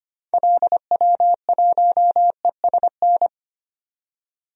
Morse code, heard as a keyed tone, about 25 wpm, 700 Hz. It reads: LW1ESN
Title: DXCC-100 Call Signs - 6 Characters - 25wpm